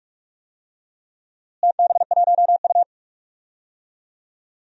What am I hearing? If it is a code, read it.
Morse code, heard as a keyed tone, about 46 wpm, 700 Hz. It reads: TB1U
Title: DXCC-100 Call Signs - 4 Characters - Repeated 2x - 45wpm